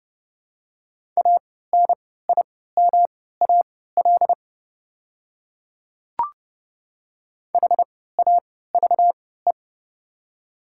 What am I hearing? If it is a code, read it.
Morse code, heard as a keyed tone, about 29 wpm, spaced as for 17 wpm, 690 Hz. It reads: ANIMAL HAVE
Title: 3x Character Spacing - Top 300 Words - 30wpm